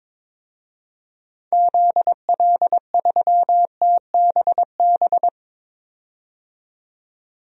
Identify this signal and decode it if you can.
Morse code, heard as a keyed tone, about 23 wpm, 700 Hz. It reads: ZL3TBB
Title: DXCC-100 Call Signs - 6 Characters - 22wpm